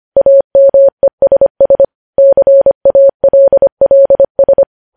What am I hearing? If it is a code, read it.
Morse code, heard as a keyed tone, about 25 wpm, 550 Hz. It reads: AMESS CALLS